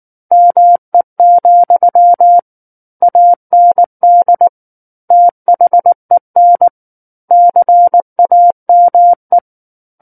Morse code, heard as a keyed tone, about 19 wpm, 700 Hz. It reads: ME, AND THEN CAME
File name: Book One / 10-Chapter 10 - 19 WPM